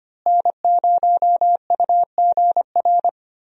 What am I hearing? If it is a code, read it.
Morse code, heard as a keyed tone, about 25 wpm, 700 Hz. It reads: N0UGR
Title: USA Call Signs - 5 Characters - Repeated 3x - 25wpm